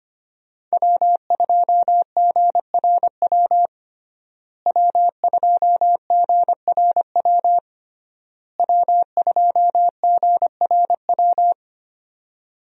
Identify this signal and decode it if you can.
Morse code, heard as a keyed tone, about 25 wpm, 700 Hz. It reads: W2GRW W2GRW W2GRW
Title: USA Call Signs - 5 Characters - Repeated 3x - 25wpm